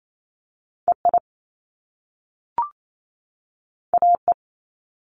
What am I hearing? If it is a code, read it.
Morse code, heard as a keyed tone, about 28 wpm, 700 Hz. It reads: EI AE
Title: Top Two Letter Combinations in Words 28wpm